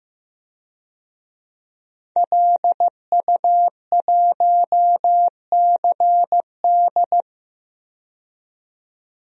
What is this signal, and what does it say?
Morse code, heard as a keyed tone, about 15 wpm, 700 Hz. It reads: LU1CD
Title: DXCC-100 Call Signs - 5 Characters - 15wpm